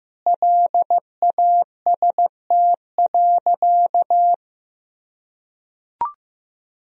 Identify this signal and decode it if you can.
Morse code, heard as a keyed tone, about 15 wpm, 700 Hz. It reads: LAST.
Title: Sentences from Top 1000 Words - Review Part 2 - 15wpm